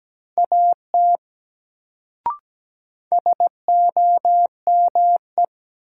Morse code, heard as a keyed tone, about 17 wpm, 700 Hz. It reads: AT SOME
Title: Sets of 2 Words - New Word Drill - 17wpm